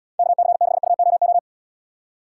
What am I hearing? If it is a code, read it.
Morse code, heard as a keyed tone, about 76 wpm, 700 Hz. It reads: KC6UOO